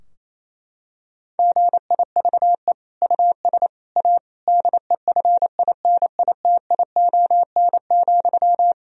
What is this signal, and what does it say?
Morse code, heard as a keyed tone, about 28 wpm, 700 Hz. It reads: GIVE US A DEFINITION,
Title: Sentences from Top 1000 Words - Review Part 4 - 28wpm